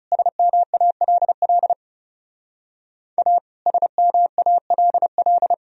Morse code, heard as a keyed tone, about 37 wpm, 700 Hz. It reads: SMALL A SMALL
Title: Speed-Racing - Sets of 2 Words 30wpm